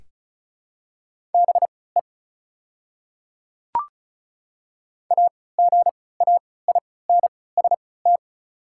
Morse code, heard as a keyed tone, about 36 wpm, spaced as for 20 wpm, 690 Hz. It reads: BE AGAINST
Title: 3x Character Spacing - Top 500 Words - 35wpm